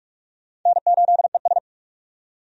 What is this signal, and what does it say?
Morse code, heard as a keyed tone, about 46 wpm, 700 Hz. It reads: N8ES
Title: USA Call Signs - 4 Characters - 45wpm